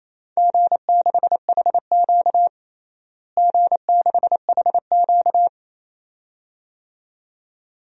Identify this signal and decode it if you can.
Morse code, heard as a keyed tone, about 28 wpm, 700 Hz. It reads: G6HQ G6HQ
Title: DXCC-100 Call Signs - 4 Characters - Repeated 2x - 28wpm